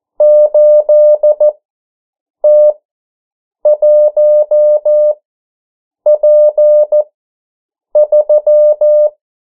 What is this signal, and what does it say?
Morse code, heard as a keyed tone, about 14 wpm, spaced as for 7 wpm, 600 Hz. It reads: 8T1P3
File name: morse mp3 / 09 urok 8